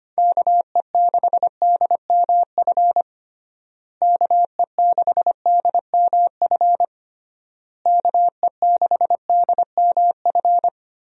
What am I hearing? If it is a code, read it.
Morse code, heard as a keyed tone, about 25 wpm, 700 Hz. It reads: KE6DMF KE6DMF KE6DMF
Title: USA Call Signs - 6 Characters - Repeated 3x - 25wpm